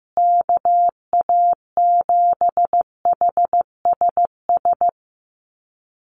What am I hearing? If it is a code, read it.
Morse code, heard as a keyed tone, about 15 wpm, 700 Hz. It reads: KA7HSS
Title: USA Call Signs Repeated 2x 15wpm